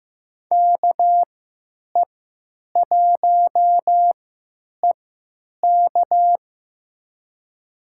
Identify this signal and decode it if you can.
Morse code, heard as a keyed tone, about 15 wpm, spaced as for 9 wpm, 700 Hz. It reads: KE1EK
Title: USA Call Signs - 5 Characters - 3x Character Spacing - 15wpm